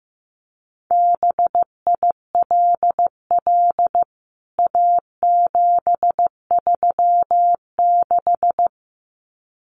Morse code, heard as a keyed tone, about 15 wpm, 700 Hz. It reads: BILL A736
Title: CWT Exchange 15wpm